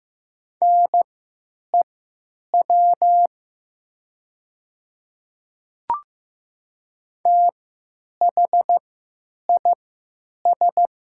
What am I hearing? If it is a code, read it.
Morse code, heard as a keyed tone, about 15 wpm, spaced as for 9 wpm, 700 Hz. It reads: NEW THIS
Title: 3x Character Spacing - Top 200 Words - 15wpm